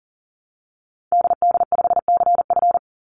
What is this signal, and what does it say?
Morse code, heard as a keyed tone, about 38 wpm, 700 Hz. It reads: DD5CF
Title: Call Signs Repeated 3x - 40wpm